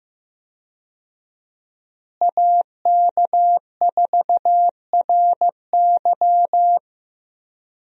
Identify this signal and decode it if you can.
Morse code, heard as a keyed tone, about 15 wpm, 700 Hz. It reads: AK4RY
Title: USA Call Signs - 5 Characters - Repeated 3x - 15wpm